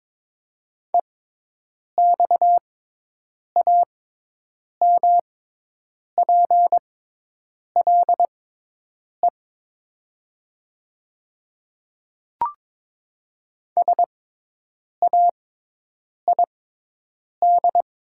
Morse code, heard as a keyed tone, about 22 wpm, spaced as for 8 wpm, 700 Hz. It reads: EXAMPLE SAID
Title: 6x Character Spacing - Top 300 Words - 22wpm